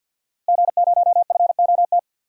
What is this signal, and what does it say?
Morse code, heard as a keyed tone, about 51 wpm, 700 Hz. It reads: G0FOT
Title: DXCC-100 Call Signs - 5 Characters - Repeated 3x - 50wpm